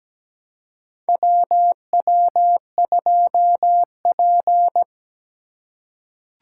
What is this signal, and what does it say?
Morse code, heard as a keyed tone, about 17 wpm, 700 Hz. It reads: WW2P
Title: K1USN SST QSO Practice - 17wpm